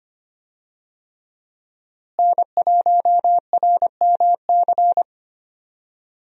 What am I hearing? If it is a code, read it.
Morse code, heard as a keyed tone, about 25 wpm, 700 Hz. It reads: N1RMC
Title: USA Call Signs - 5 Characters - Repeated 3x - 25wpm